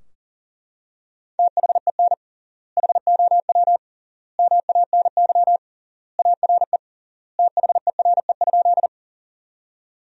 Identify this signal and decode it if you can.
Morse code, heard as a keyed tone, about 40 wpm, 700 Hz. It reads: THEN HOW MANY ARE THERE?
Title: 3x Word-Spacing - Sentences from Top 100 Words - 40wpm